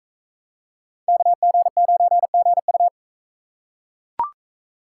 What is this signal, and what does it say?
Morse code, heard as a keyed tone, about 44 wpm, 700 Hz. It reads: KG9GU